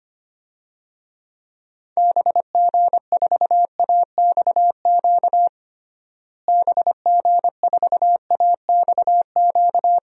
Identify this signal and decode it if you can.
Morse code, heard as a keyed tone, about 25 wpm, 700 Hz. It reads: BG4AXQ BG4AXQ
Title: DXCC-100 Call Signs - 6 Characters - Repeated 2x - 25wpm